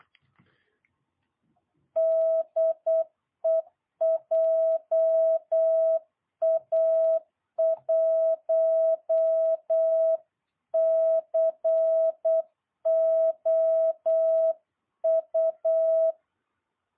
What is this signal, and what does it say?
Morse code, heard as a keyed tone, about 8 wpm, 660 Hz. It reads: DE JA1COU